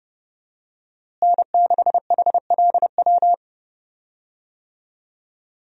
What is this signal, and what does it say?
Morse code, heard as a keyed tone, about 30 wpm, 700 Hz. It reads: N6HLW
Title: USA Call Signs - 5 Characters - Repeated 3x - 30wpm